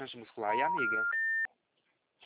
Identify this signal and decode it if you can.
Special information tone, with speech.
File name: out- -2038- - - .9067